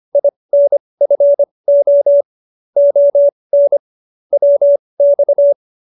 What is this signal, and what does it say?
Morse code, heard as a keyed tone, about 25 wpm, spaced as for 20 wpm, 570 Hz. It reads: INFO ON WX